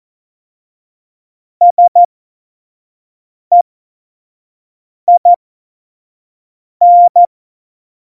Morse code, heard as a keyed tone, about 14 wpm, spaced as for 5 wpm, 700 Hz. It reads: SEIN
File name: Morsekurs / Witze / Witz 23. Januar 2018, 25 F70